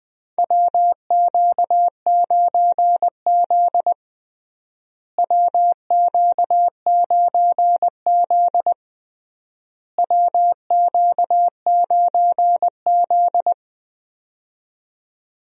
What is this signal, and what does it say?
Morse code, heard as a keyed tone, about 21 wpm, 700 Hz. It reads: WQ9Z WQ9Z WQ9Z
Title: USA Call Signs - 4 Characters - Repeated 3x - 20wpm